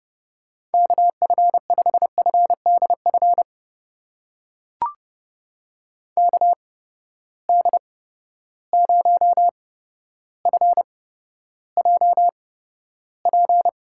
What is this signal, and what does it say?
Morse code, heard as a keyed tone, about 31 wpm, spaced as for 13 wpm, 700 Hz. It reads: KF5FDF KD0FJP
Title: USA Call Signs - 6 Characters - 8x Character Spacing - 30wpm